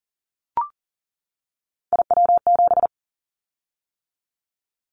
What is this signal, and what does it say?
Morse code, heard as a keyed tone, about 38 wpm, 700 Hz. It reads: IW7
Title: Callsign Prefixes for Top 25 DXCC - 40wpm